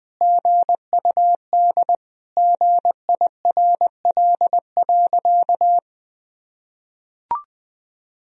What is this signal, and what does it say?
Morse code, heard as a keyed tone, about 20 wpm, 700 Hz. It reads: GUD GIRL.
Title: Sentences from Top 400 Words and Abbreviations 20wpm